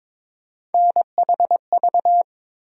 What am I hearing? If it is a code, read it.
Morse code, heard as a keyed tone, about 22 wpm, 700 Hz. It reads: NHV
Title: USA Call Sign Postfixes - 3 Letters 22wpm